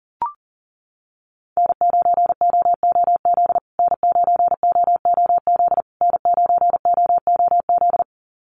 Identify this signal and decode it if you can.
Morse code, heard as a keyed tone, about 40 wpm, 700 Hz. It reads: N9OOZ N9OOZ N9OOZ
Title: USA Call Signs Repeated 3x 40wpm